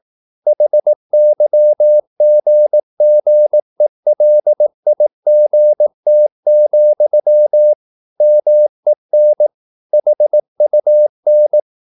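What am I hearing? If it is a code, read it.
Morse code, heard as a keyed tone, about 18 wpm, 600 Hz. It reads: HYGGELIGT, MEN HUN
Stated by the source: Slow (QRS) Morse practice at 18 wpm, CW med Gnister.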